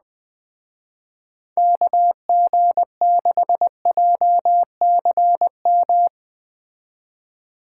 Morse code, heard as a keyed tone, about 20 wpm, 700 Hz. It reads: KG6JCM